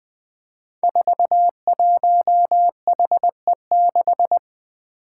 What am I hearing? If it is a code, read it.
Morse code, heard as a keyed tone, about 20 wpm, 700 Hz. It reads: 41HE6
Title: Five Challenging Letter-Number Combinations 20wpm